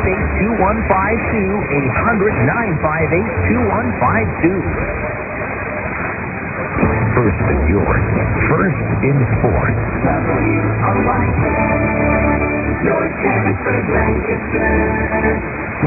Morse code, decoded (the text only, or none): none